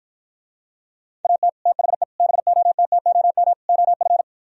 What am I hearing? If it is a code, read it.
Morse code, heard as a keyed tone, about 53 wpm, 700 Hz. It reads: AT THE BOTTOM OF